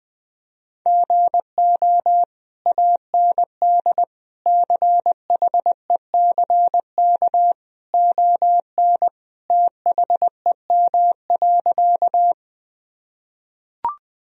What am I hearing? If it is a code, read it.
Morse code, heard as a keyed tone, about 20 wpm, 700 Hz. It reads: GO AND CHECK ON THEM.
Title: Sentences from Top 1000 Words - Review Part 3 - 20wpm